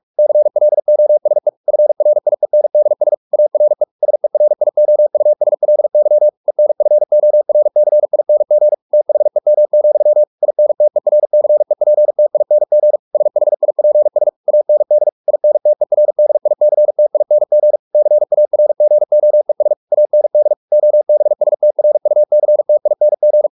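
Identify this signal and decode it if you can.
Morse code, heard as a keyed tone, about 44 wpm, 600 Hz. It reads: CLOSE FRIENDS ARE SERIOUSLY ENFORCING THEM, INTERCEPTING SHIPS AND INTERDICTING CARGOES AND OBSTRUCTING
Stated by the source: Morse practice at 45 wpm, the novel Never by Ken Follett in English, from CW med Gnister.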